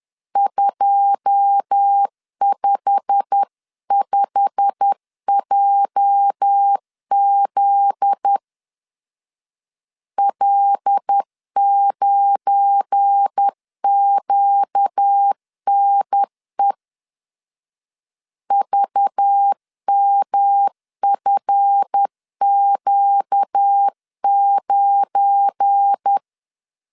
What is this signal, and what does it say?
Morse code, heard as a keyed tone, about 10 wpm, 800 Hz. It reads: 255JZ L9QNE VMFQ9